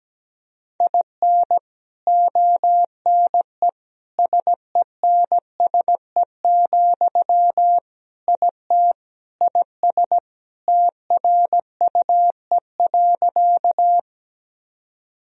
Morse code, heard as a keyed tone, about 17 wpm, 700 Hz. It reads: IN ONE SENSE, IT IS TRUE.
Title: Sentences from Top 1000 Words - Review Part 4 - 17wpm